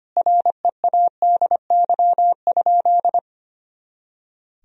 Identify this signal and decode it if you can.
Morse code, heard as a keyed tone, about 25 wpm, 700 Hz. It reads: READY?
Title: Sentences from Top 400 Words and Abbreviations 25wpm